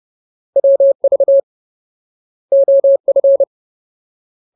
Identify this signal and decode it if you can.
Morse code, heard as a keyed tone, about 30 wpm, 550 Hz. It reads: WV OF
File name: Ultimate Word List / 30WPM Ultimate